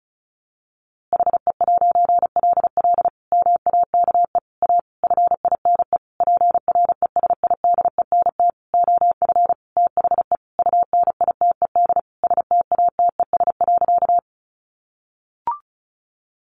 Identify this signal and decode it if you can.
Morse code, heard as a keyed tone, about 34 wpm, 700 Hz. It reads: HE'LL MAKE A FINE PRESIDENT OF THE UNITED STATES.
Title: A Sound Of Thunder 35wpm